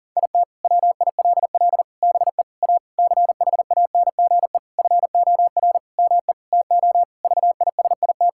Morse code, heard as a keyed tone, about 40 wpm, 700 Hz. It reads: IT WILL BE A CHANGE FOR ME TO VISIT
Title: Sentences from Top 1000 Words - Review Part 6 - 40wpm